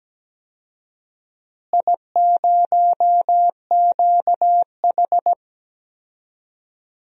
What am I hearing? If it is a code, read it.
Morse code, heard as a keyed tone, about 17 wpm, 700 Hz. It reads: I0QH
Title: DXCC-100 Call Signs - 4 Characters - Repeated 3x - 17wpm